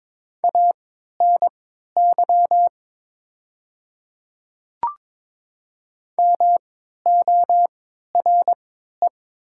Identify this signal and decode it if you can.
Morse code, heard as a keyed tone, about 22 wpm, spaced as for 13 wpm, 700 Hz. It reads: ANY MORE
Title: 3x Character Spacing - Top 300 Words - 22wpm